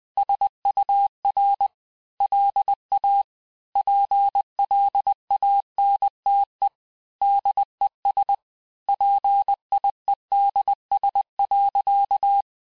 Morse code, heard as a keyed tone, about 20 wpm, 800 Hz. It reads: SUR LA PLANTE DES PIEDS.
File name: cours-du-mercredi / audios / cours-858-texte-clais-vitesse-19mpm